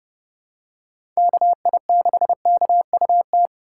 Morse code, heard as a keyed tone, about 30 wpm, 700 Hz. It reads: KI6KUT
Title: USA Call Signs - 6 Characters - 3x Character Spacing - 30wpm